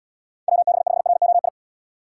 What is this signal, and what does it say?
Morse code, heard as a keyed tone, about 72 wpm, 700 Hz. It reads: WB5ROI